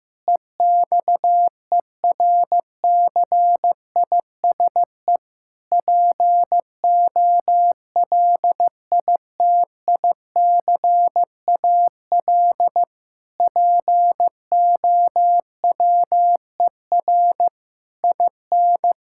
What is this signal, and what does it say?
Morse code, heard as a keyed tone, about 15 wpm, 700 Hz. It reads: EXERCISE POLITICAL POWER IN
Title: Sentences from Top 1000 Words - Review Part 6 - 15wpm